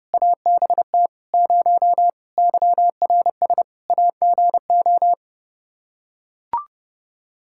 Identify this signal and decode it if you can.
Morse code, heard as a keyed tone, about 30 wpm, 700 Hz. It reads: ABT 0 YRS AGO
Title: K1USN SST QSO Practice 30wpm